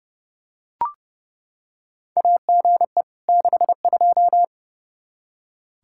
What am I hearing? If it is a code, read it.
Morse code, heard as a keyed tone, about 30 wpm, 700 Hz. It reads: AGE 62
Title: K1USN SST QSO Practice 30wpm